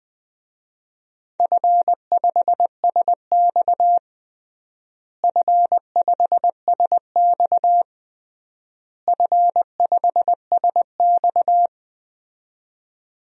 Morse code, heard as a keyed tone, about 21 wpm, 700 Hz. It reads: F5SX F5SX F5SX
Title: DXCC-100 Call Signs - 4 Characters - Repeated 3x - 20wpm